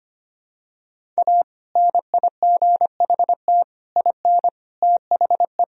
Morse code, heard as a keyed tone, about 25 wpm, 700 Hz. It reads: A NIGHT IN THE
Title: Sentences from Top 1000 Words - Review Part 1 - 25wpm